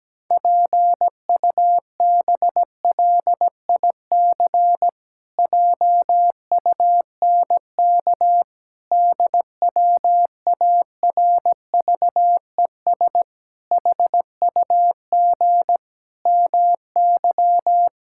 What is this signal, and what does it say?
Morse code, heard as a keyed tone, about 17 wpm, 700 Hz. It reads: PUBLIC JUNK DWARVES HUG MY
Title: Pangram Challenge v2 - 17wpm